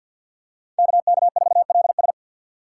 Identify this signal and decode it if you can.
Morse code, heard as a keyed tone, about 46 wpm, 700 Hz. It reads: KK4LS
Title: USA Call Signs - 3x Character Spacing - 50wpm